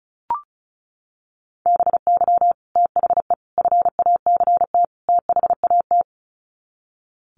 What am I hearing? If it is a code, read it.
Morse code, heard as a keyed tone, about 34 wpm, 700 Hz. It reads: BY THE FACT THAT
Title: Sets of 4 Words 35wpm